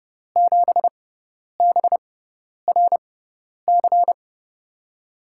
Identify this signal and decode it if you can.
Morse code, heard as a keyed tone, about 30 wpm, spaced as for 10 wpm, 700 Hz. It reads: 7BRC
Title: USA Call Signs - 5 Characters - 6x Character Spacing - 30wpm